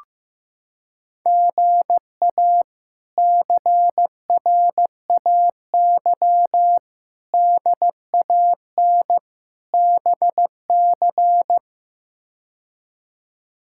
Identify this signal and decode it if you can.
Morse code, heard as a keyed tone, about 15 wpm, 700 Hz. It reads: GA CRAY DAN BC
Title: K1USN SST QSO Practice - 15wpm